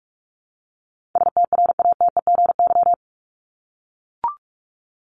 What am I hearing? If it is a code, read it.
Morse code, heard as a keyed tone, about 44 wpm, 700 Hz. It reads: STRATEGY
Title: Top 501-750 Words 45wpm